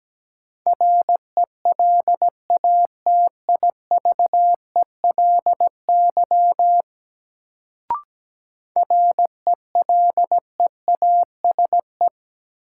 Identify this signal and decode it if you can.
Morse code, heard as a keyed tone, about 17 wpm, 700 Hz. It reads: RELATIVELY RELEASE
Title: Sentences from Top 1100 Words - New Word Drill Alt - 17wpm